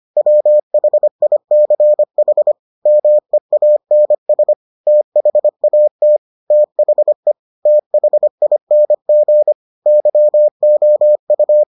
Morse code, heard as a keyed tone, about 25 wpm, 600 Hz. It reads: WHICH MEANS THAT THE THING YOU